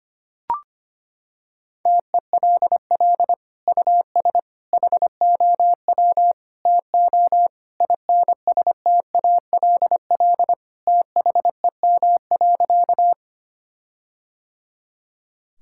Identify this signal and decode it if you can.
Morse code, heard as a keyed tone, about 25 wpm, 700 Hz. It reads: TELL US HOW TO INSTALL THEM.
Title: Sentences from Top 1600 Words 25wpm